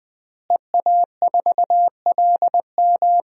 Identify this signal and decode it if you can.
Morse code, heard as a keyed tone, about 20 wpm, 700 Hz. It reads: EA4LM
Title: DXCC-100 Call Signs - 5 Characters - Repeated 3x - 20wpm